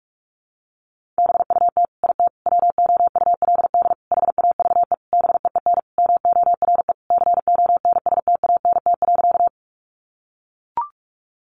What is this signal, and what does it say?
Morse code, heard as a keyed tone, about 44 wpm, 700 Hz. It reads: BUT IT WOULD HAVE BEEN MORE CONSTANT.
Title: Sentences from Top 1500 Words 45wpm